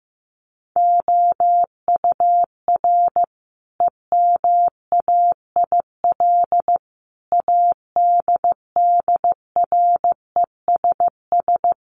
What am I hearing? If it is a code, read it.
Morse code, heard as a keyed tone, about 15 wpm, 700 Hz. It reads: OUR EMAIL ADDRESS
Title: Sets of 5 Words Encore2 15wpm